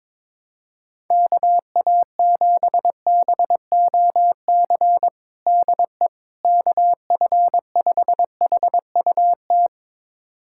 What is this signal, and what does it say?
Morse code, heard as a keyed tone, about 21 wpm, 700 Hz. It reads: KA7BOC DE KF5HUT
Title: USA Callsign DE Callsign - 6 Characters - 22wpm